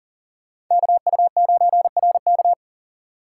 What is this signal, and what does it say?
Morse code, heard as a keyed tone, about 40 wpm, 700 Hz. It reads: KU9RK
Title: K1USN SST QSO Practice - 40wpm